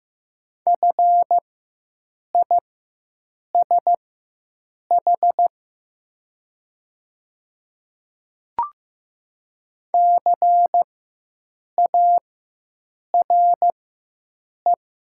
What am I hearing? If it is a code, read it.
Morse code, heard as a keyed tone, about 15 wpm, spaced as for 7 wpm, 700 Hz. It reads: FISH CARE
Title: 4x Character Spacing - Top 300 Words - 15wpm